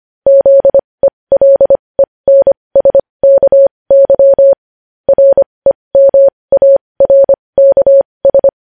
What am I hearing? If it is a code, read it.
Morse code, heard as a keyed tone, about 25 wpm, spaced as for 20 wpm, 550 Hz. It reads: ZELENSKY REMARKS
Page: news-headlines-in-morse-code-at-20-wpm